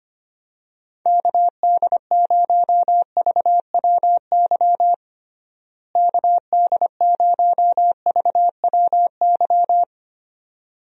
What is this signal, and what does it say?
Morse code, heard as a keyed tone, about 25 wpm, 700 Hz. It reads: KD0VWY KD0VWY
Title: USA Call Signs - 6 Characters - Repeated 2x - 25wpm